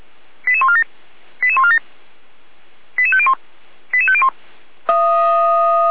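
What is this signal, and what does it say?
Five-tone selective call (ZVEI), heard as quick runs of five tones, tones of about 70 ms. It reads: 88147 88147 88501 88501
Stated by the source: ZVEI five-tone selective call received by SDR.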